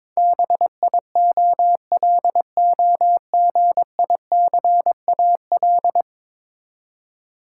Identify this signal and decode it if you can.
Morse code, heard as a keyed tone, about 22 wpm, 700 Hz. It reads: BIOLOGICAL
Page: morse-code-ninja-practice-22wpm